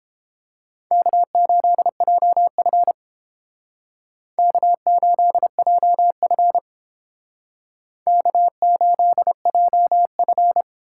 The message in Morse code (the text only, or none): K8JF K8JF K8JF